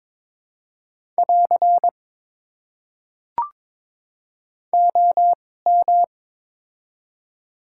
Morse code, heard as a keyed tone, about 22 wpm, spaced as for 16 wpm, 700 Hz. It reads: <AR> OM
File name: Top 90 QSO Elements - 2x Character Spacing - 22wpm